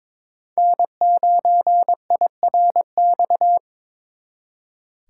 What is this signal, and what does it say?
Morse code, heard as a keyed tone, about 26 wpm, 700 Hz. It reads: N9IRX
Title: Speed Racing - USA 1x3 Call Signs 22wpm